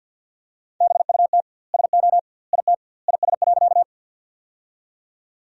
Morse code, heard as a keyed tone, about 50 wpm, 700 Hz. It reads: BUT SO IT IS.
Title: 2x Word-Spacing - Sentences from Top 100 Words - 50wpm